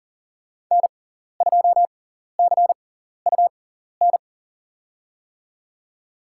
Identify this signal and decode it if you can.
Morse code, heard as a keyed tone, about 40 wpm, spaced as for 14 wpm, 700 Hz. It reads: N2CUN
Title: USA Call Signs - 6x Character Spacing - 40wpm